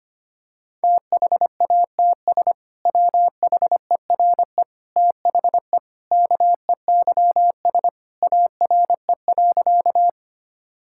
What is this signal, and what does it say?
Morse code, heard as a keyed tone, about 25 wpm, 700 Hz. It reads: THATS WHERE THE KEYS ARE.
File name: practice / 25wpm / Sentences with 2-Word Phrases - 25wpm